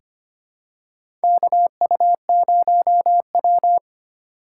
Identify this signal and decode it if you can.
Morse code, heard as a keyed tone, about 25 wpm, 700 Hz. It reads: KU0W
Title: USA Call Signs - 4 Characters - 6x Character Spacing - 25wpm